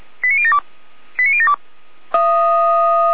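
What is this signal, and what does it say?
Five-tone selective call (ZVEI), heard as quick runs of five tones, tones of about 70 ms. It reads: 89072 89072
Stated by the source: ZVEI five-tone call received by SDR, recorded by Feuerwehr Orlamünde.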